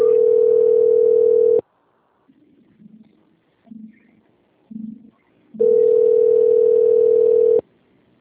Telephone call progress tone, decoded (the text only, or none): ringback tone